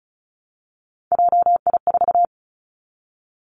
Morse code, heard as a keyed tone, about 36 wpm, 700 Hz. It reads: JI4
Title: Three Challenging Letter-Number Combinations 35wpm